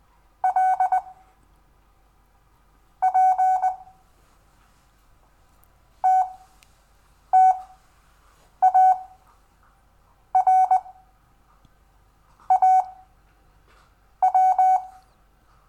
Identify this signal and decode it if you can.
Morse code, heard as a keyed tone, about 20 wpm, spaced as for 6 wpm, 750 Hz. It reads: L P TTARAW
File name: cw / Lesson6 WI